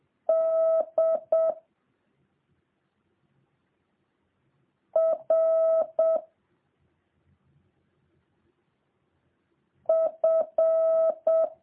Morse code, heard as a keyed tone, about 7 wpm, 650 Hz. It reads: D R F